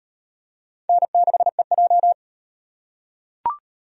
Morse code, heard as a keyed tone, about 38 wpm, 700 Hz. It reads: N6EJ